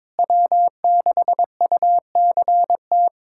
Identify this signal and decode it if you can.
Morse code, heard as a keyed tone, about 22 wpm, 700 Hz. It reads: W6UCT
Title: USA Callsign DE Callsign - 5 Characters - 22wpm